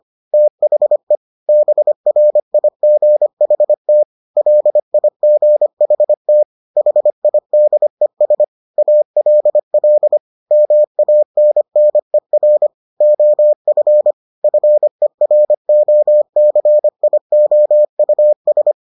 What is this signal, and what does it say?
Morse code, heard as a keyed tone, about 25 wpm, 600 Hz. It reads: THE BRIGHT LIGHT HIDES ALL MANNER OF FEROCIOUS